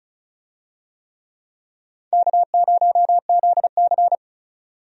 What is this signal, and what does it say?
Morse code, heard as a keyed tone, about 34 wpm, 700 Hz. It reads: K0ZC
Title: USA Call Signs - 4 Characters - 35wpm